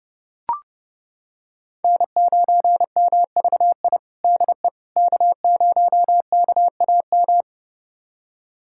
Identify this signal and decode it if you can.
Morse code, heard as a keyed tone, about 30 wpm, 700 Hz. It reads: N9MVI DE K0KAM
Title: USA Callsign DE Callsign - 5 Characters - 30wpm